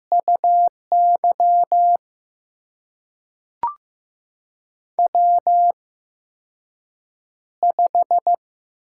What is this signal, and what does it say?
Morse code, heard as a keyed tone, about 15 wpm, spaced as for 6 wpm, 700 Hz. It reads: UY W5
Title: USA Call Signs - 4 Characters - 8x Character Spacing - 15wpm